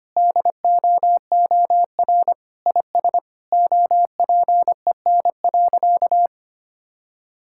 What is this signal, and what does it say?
Morse code, heard as a keyed tone, about 25 wpm, 700 Hz. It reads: DOOR IS OPEN.
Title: Sentences from Top 1600 Words 25wpm